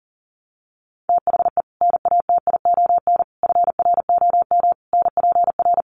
Morse code, heard as a keyed tone, about 40 wpm, 700 Hz. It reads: THE NATION FROM NPR
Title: Sets of 5 Words 40wpm